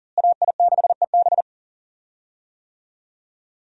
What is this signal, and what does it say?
Morse code, heard as a keyed tone, about 40 wpm, 700 Hz. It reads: AI6EB
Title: USA Call Signs - 5 Characters - Repeated 3x - 40wpm